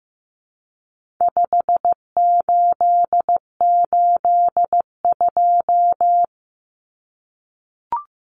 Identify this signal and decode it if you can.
Morse code, heard as a keyed tone, about 15 wpm, 700 Hz. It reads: 5882
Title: Numbers - 4-Digit 15wpm